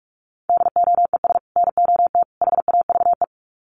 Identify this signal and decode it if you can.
Morse code, heard as a keyed tone, about 44 wpm, 700 Hz. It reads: DOES NOT HAVE
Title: Sets of 3 Words 45wpm